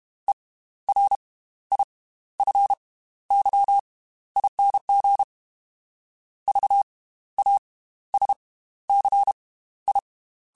Morse code, heard as a keyed tone, about 32 wpm, 800 Hz. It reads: E R I F Y ING V A S C I